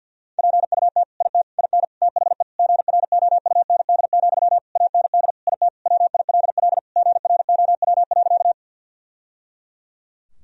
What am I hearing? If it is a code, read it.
Morse code, heard as a keyed tone, about 47 wpm, 700 Hz. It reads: PUT IT IN THE GROUND, AND IT WILL GROW.
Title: Sentences from Top 1000 Words - Review Part 2 - 50wpm